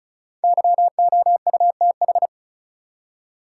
Morse code, heard as a keyed tone, about 36 wpm, 700 Hz. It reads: YOUTH
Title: Five Letter Words - 35wpm